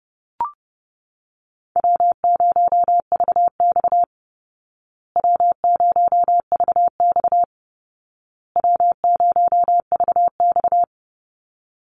Morse code, heard as a keyed tone, about 30 wpm, 700 Hz. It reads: W0VX W0VX W0VX
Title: Easy Call Signs Repeated 3x - 30wpm